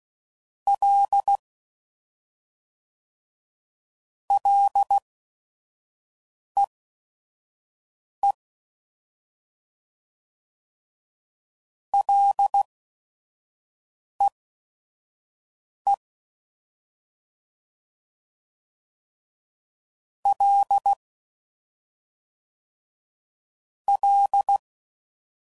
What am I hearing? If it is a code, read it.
Morse code, heard as a keyed tone, about 16 wpm, spaced as for 5 wpm, 800 Hz. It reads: L LEE LEE L L
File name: Prefixes / FLs & THs / Single L's